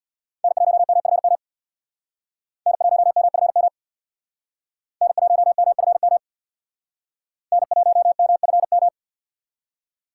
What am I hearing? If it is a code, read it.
Morse code, heard as a keyed tone, about 61 wpm, 700 Hz. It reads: N1MFM N1MFM N1MFM N1MFM
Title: Speed Racing - USA 1x3 Call Signs 50wpm